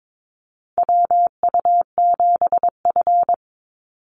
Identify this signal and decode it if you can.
Morse code, heard as a keyed tone, about 22 wpm, 700 Hz. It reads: WU7F
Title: Call Signs - 22wpm